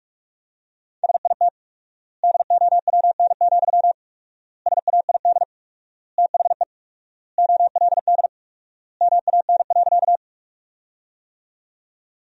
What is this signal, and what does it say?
Morse code, heard as a keyed tone, about 47 wpm, 700 Hz. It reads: SIT DOWN, SAID THE OLD MAN.
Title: 4x Word-Spacing - Sentences from Top 500 Words - 45wpm